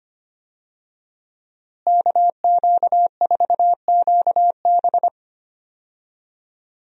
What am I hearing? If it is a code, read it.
Morse code, heard as a keyed tone, about 25 wpm, 700 Hz. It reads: KQ4QB
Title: USA Call Signs - 5 Characters - Repeated 2x - 25wpm